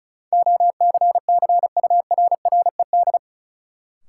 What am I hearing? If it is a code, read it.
Morse code, heard as a keyed tone, about 34 wpm, 700 Hz. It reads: OCCURRED
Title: Sets of 5 Words Encore - New Word Drill - 35wpm